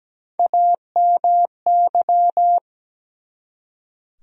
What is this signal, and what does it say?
Morse code, heard as a keyed tone, about 17 wpm, 700 Hz. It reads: AMY
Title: USA Common Names - Max 4 Letters - 17wpm